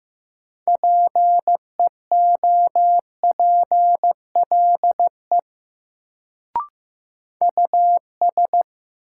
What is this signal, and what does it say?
Morse code, heard as a keyed tone, about 15 wpm, 700 Hz. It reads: PEOPLE US
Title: Sets of 2 Words - New Word Drill - 15wpm